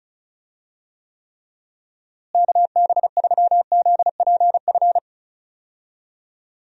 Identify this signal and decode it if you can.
Morse code, heard as a keyed tone, about 36 wpm, 700 Hz. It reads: KB3ZPF
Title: USA Call Signs - 6 Characters - Repeated 3x - 35wpm